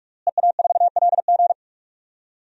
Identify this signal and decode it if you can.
Morse code, heard as a keyed tone, about 46 wpm, 700 Hz. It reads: EA4LG
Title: DXCC-100 Call Signs - 5 Characters - Repeated 3x - 45wpm